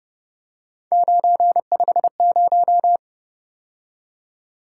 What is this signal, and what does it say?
Morse code, heard as a keyed tone, about 30 wpm, 700 Hz. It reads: 950
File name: Speed-Racing - Three Letter-Number Combinations - 25wpm